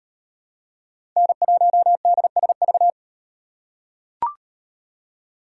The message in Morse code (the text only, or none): N1DSV